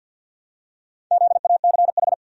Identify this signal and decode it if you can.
Morse code, heard as a keyed tone, about 51 wpm, 700 Hz. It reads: ZACH
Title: USA Common Names - Max 4 Letters - 50wpm